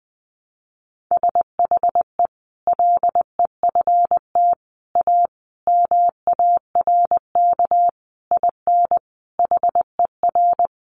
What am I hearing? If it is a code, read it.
Morse code, heard as a keyed tone, about 20 wpm, 700 Hz. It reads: SHE LEFT A MARK IN HER
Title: Sentences from Top 1100 Words 20wpm